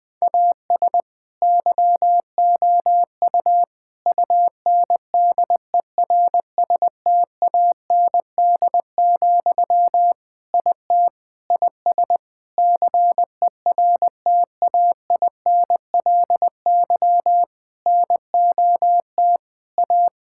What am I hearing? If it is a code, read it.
Morse code, heard as a keyed tone, about 21 wpm, 700 Hz. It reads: AS YOU UNDERSTAND, IT IS CERTAINLY NOT A
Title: Sentences from Top 1000 Words - Review Part 5 - 20wpm